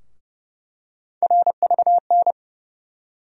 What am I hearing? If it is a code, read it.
Morse code, heard as a keyed tone, about 30 wpm, 700 Hz. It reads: RVN
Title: USA Call Sign Postfixes - 3 Letters 30wpm